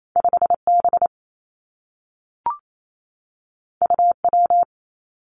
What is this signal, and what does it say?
Morse code, heard as a keyed tone, about 28 wpm, 700 Hz. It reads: 5 B U W
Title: Two Challenging Letter-Number Combinations 28wpm